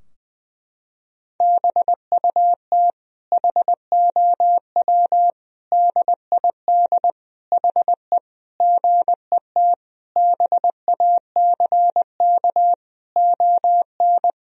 Morse code, heard as a keyed tone, about 20 wpm, 700 Hz. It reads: BUT HOW DID HE GET BACK ON
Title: Sentences from Top 1000 Words - Review Part 4 - 20wpm